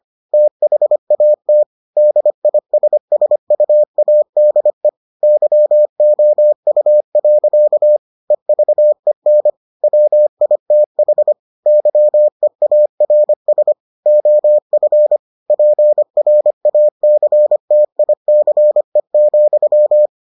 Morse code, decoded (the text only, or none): THAT DISSUADE YOU. EVEN WITH YEARS OF PRACTICE,